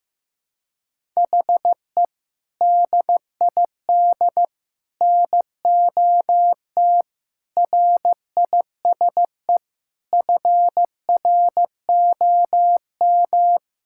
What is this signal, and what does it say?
Morse code, heard as a keyed tone, about 15 wpm, 700 Hz. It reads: HE DID NOT RISE FROM
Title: Sentences from Top 1000 Words - Review Part 6 - 15wpm